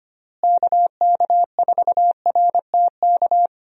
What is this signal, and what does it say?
Morse code, heard as a keyed tone, about 25 wpm, 700 Hz. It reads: KK4RTK
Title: USA Call Signs - 6 Characters - Repeated 2x - 25wpm